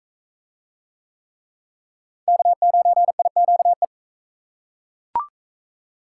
Morse code, heard as a keyed tone, about 41 wpm, 700 Hz. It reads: K9IQE